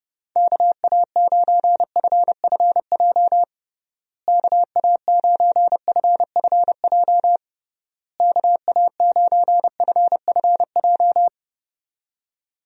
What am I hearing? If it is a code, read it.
Morse code, heard as a keyed tone, about 30 wpm, 700 Hz. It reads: KA9FFJ KA9FFJ KA9FFJ
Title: USA Call Signs - 6 Characters - Repeated 3x - 30wpm